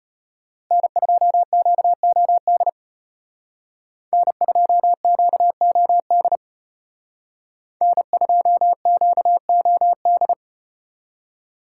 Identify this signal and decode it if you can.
Morse code, heard as a keyed tone, about 32 wpm, 700 Hz. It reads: N2QOD N2QOD N2QOD